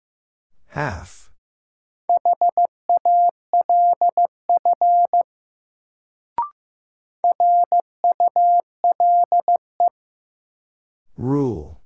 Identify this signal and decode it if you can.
Morse code, heard as a keyed tone, about 15 wpm, 700 Hz. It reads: HALF RULE
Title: Sentences from Top 400 Words - New Word Drill - 15wpm